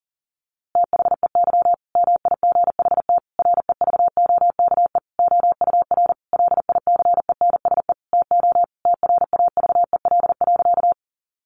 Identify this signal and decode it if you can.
Morse code, heard as a keyed tone, about 40 wpm, 700 Hz. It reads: THEY MIGHT REVOKE OUR LICENSE TO TRAVEL.
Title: A Sound Of Thunder 40wpm